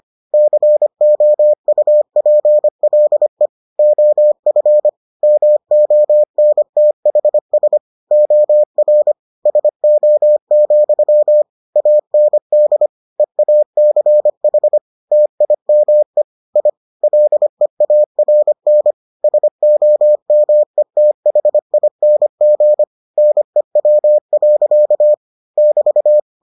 Morse code, heard as a keyed tone, about 25 wpm, 600 Hz. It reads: COUPLE OF MONTHS OR SO, AND EACH TIME I LEARN SOMETHING NEW. <BT>